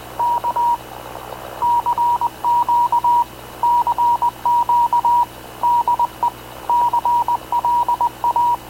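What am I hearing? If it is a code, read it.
Morse code, heard as a keyed tone, about 19 wpm, 950 Hz. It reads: K CQ CQ DE CLA
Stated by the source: Havana Radio / CLA sending Spanish press (PX) in Morse, recorded by Radio Officer Paul Dumesnil onboard CNAV QUEST/CZDO.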